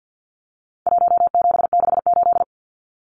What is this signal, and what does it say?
Morse code, heard as a keyed tone, about 48 wpm, 700 Hz. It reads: 1768
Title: Numbers - Contest Style - 4-Digit 50wpm